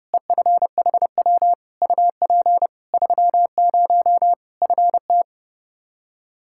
Morse code, heard as a keyed tone, about 30 wpm, 700 Hz. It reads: EFHW UP 30 FT
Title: Practice QSO - Exchange 2 - Info - 30wpm